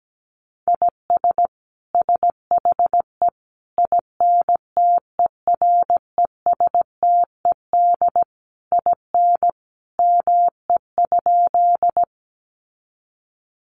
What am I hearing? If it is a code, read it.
Morse code, heard as a keyed tone, about 17 wpm, 700 Hz. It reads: IS SHE INTERESTED IN ME?
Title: Sentences from Top 900 Words 17wpm